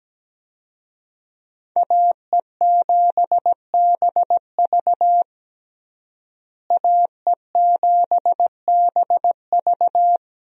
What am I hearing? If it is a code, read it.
Morse code, heard as a keyed tone, about 17 wpm, 700 Hz. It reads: AE7BV AE7BV
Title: USA Call Signs - 5 Characters - Repeated 3x - 17wpm